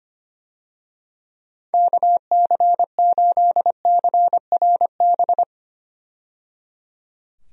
Morse code, heard as a keyed tone, about 25 wpm, 700 Hz. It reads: KC8CRB
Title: USA Call Signs - 6 Characters - 25wpm